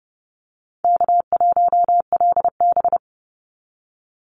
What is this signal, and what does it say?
Morse code, heard as a keyed tone, about 30 wpm, 700 Hz. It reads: K1LB
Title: NA SOTA Activator Callsigns 30wpm